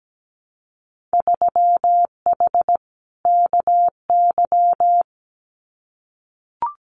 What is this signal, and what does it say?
Morse code, heard as a keyed tone, about 17 wpm, 700 Hz. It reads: 3H KY
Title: Winter Field Day Exchanges Encore 17wpm